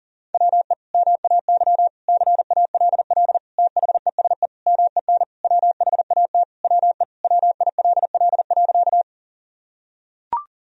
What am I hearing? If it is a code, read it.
Morse code, heard as a keyed tone, about 40 wpm, 700 Hz. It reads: WE MAY CALL THESE MEN WHAT WE WILL.
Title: Sentences from Top 1000 Words - Review Part 1 - 40wpm